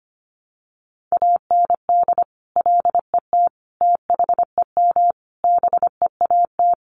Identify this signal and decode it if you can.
Morse code, heard as a keyed tone, about 25 wpm, 700 Hz. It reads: AND LET THEM BEAT
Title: Sentences from Top 1300 Words 25wpm